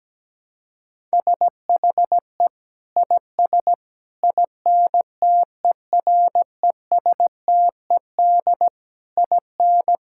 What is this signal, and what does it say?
Morse code, heard as a keyed tone, about 17 wpm, 700 Hz. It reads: SHE IS INTERESTED IN
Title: Sentences from Top 1000 Words - Review Part 6 - 17wpm